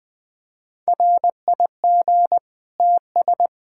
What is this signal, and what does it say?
Morse code, heard as a keyed tone, about 20 wpm, 700 Hz. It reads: RIG TS
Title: K1USN SST QSO Practice 20wpm